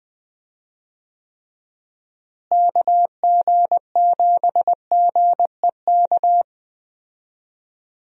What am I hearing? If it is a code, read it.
Morse code, heard as a keyed tone, about 20 wpm, 700 Hz. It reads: KG7GEK
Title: USA Call Signs - 6 Characters - Repeated 2x - 20wpm